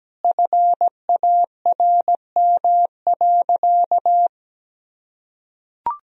Morse code, heard as a keyed tone, about 17 wpm, 700 Hz. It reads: FARM.
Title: Sentences from Top 1000 Words - Review Part 2 - 17wpm